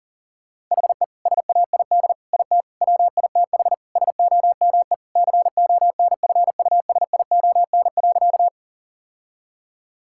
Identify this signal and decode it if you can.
Morse code, heard as a keyed tone, about 40 wpm, 700 Hz. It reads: HE SAID IT WITH SOME CONFUSION.
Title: Sentences from Top 1600 Words 40wpm